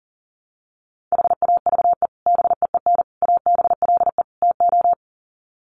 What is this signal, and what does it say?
Morse code, heard as a keyed tone, about 40 wpm, 700 Hz. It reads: HAVE BEEN ABLE TO
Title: Sets of 4 Words 40wpm